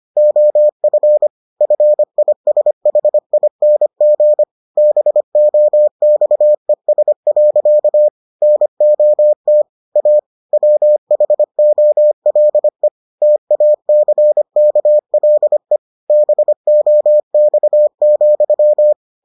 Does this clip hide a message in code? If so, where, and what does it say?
Yes: Morse code from 0.2 to 18.9 s, heard as a keyed tone, about 25 wpm, 600 Hz. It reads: OF FISHING BOXES. NOT A WHOLE TACKLE BOX,